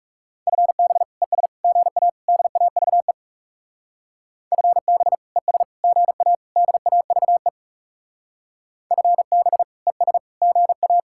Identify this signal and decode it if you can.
Morse code, heard as a keyed tone, about 35 wpm, 700 Hz. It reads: FB ES GA DAVE FB ES GA DAVE FB ES GA
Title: Speed-Racing Common QSO Phrases - 30wpm